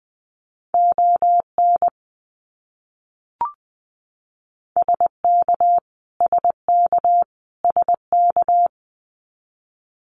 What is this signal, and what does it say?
Morse code, heard as a keyed tone, about 19 wpm, 700 Hz. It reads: ON SK SK SK
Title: Canadian Provinces and Territories - Repeated 3x - 20wpm